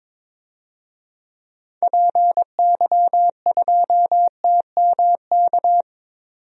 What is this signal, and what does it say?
Morse code, heard as a keyed tone, about 22 wpm, 700 Hz. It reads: PY2TMK
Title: DXCC-100 Call Signs - 6 Characters - 22wpm